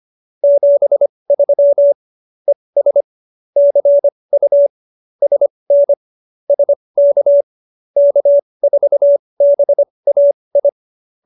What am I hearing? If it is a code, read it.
Morse code, heard as a keyed tone, about 25 wpm, spaced as for 20 wpm, 570 Hz. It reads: 73 ES CU SN SK K4BAI
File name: QSO301 20